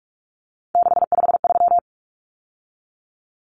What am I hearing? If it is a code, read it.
Morse code, heard as a keyed tone, about 44 wpm, 700 Hz. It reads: 653
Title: Numbers - 3-Digit 45wpm